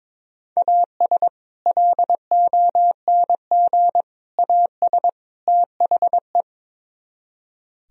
Morse code, heard as a keyed tone, about 22 wpm, 700 Hz. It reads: AS LONG AS THE
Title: Speed-Racing - Sets of 4 Words 22wpm